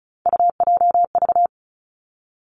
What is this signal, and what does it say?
Morse code, heard as a keyed tone, about 36 wpm, 700 Hz. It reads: UJV
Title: Three Challenging Letter-Number Combinations 35wpm